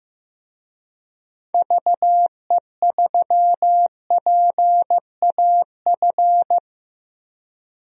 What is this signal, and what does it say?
Morse code, heard as a keyed tone, about 15 wpm, 700 Hz. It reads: VE3PAF
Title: DXCC-100 Call Signs - 6 Characters - Repeated 3x - 15wpm